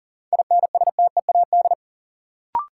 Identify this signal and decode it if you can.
Morse code, heard as a keyed tone, about 40 wpm, 700 Hz. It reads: INSTEAD
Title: Sentences from Top 800 Words - New Word Drill - 40wpm